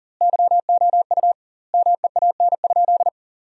Morse code, heard as a keyed tone, about 40 wpm, 700 Hz. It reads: YOU MEAN?
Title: 2x Word-Spacing - Sentences from Top 200 Words - 40wpm